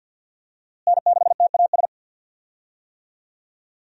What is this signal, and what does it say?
Morse code, heard as a keyed tone, about 51 wpm, 700 Hz. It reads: N6TAS
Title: USA Call Signs - 5 Characters - Repeated 2x - 50wpm